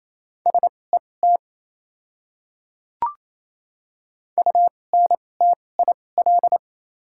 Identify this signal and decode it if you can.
Morse code, heard as a keyed tone, about 28 wpm, spaced as for 20 wpm, 700 Hz. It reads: SET UNTIL
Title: 2x Character Spacing - Top 300 Words - 28wpm